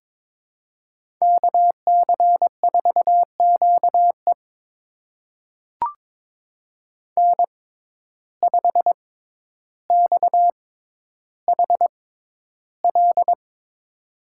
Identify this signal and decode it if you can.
Morse code, heard as a keyed tone, about 22 wpm, spaced as for 11 wpm, 700 Hz. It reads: KC4QE N5XHL
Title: USA Call Signs - 5 Characters - 6x Character Spacing - 22wpm